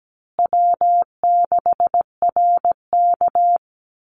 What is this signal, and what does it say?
Morse code, heard as a keyed tone, about 17 wpm, 700 Hz. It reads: W6RK
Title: Easy Call Signs Repeated 3x - 17wpm